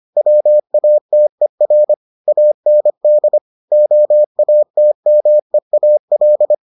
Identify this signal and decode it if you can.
Morse code, heard as a keyed tone, about 25 wpm, 600 Hz. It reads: WATER AND OATMEAL